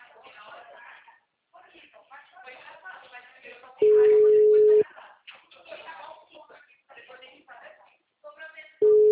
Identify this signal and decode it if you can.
Ringback tone (Europe, 425 Hz), with speech.